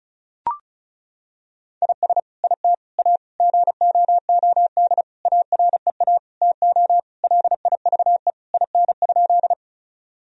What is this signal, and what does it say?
Morse code, heard as a keyed tone, about 36 wpm, 700 Hz. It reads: IS IT A GOOD AREA TO LIVE IN?
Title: Sentences from Top 1000 Words - Review Part 2 - 35wpm